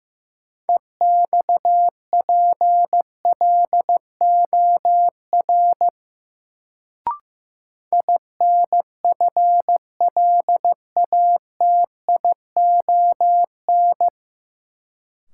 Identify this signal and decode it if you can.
Morse code, heard as a keyed tone, about 15 wpm, 700 Hz. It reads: EXPLOR INFLATION
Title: Sentences from Top 1100 Words - New Word Drill - 15wpm